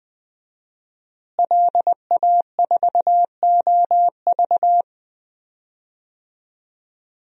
Morse code, heard as a keyed tone, about 21 wpm, 700 Hz. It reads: LA4OV
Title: DXCC-100 Call Signs - 5 Characters - Repeated 3x - 20wpm